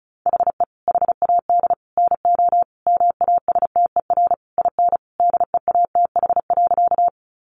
Morse code, heard as a keyed tone, about 34 wpm, 700 Hz. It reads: HE HAD NO MASTER IN DEATH.
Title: Sentences from Top 1300 Words 35wpm